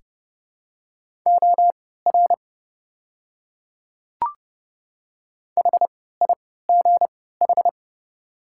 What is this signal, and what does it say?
Morse code, heard as a keyed tone, about 31 wpm, spaced as for 17 wpm, 700 Hz. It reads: OR HIGH
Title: 3x Character Spacing - Top 200 Words - 30wpm